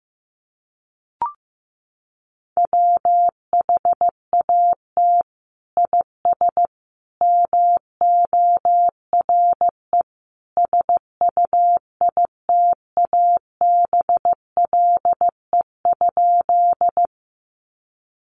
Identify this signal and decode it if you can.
Morse code, heard as a keyed tone, about 15 wpm, 700 Hz. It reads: WHAT IS MORE SUITABLE?
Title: Sentences from Top 1400 Words 15wpm